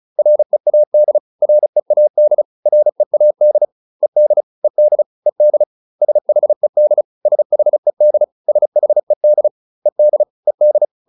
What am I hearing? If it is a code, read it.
Morse code, heard as a keyed tone, about 34 wpm, 600 Hz. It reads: READ READ READ ED ED ED SHED SHED SHED ED ED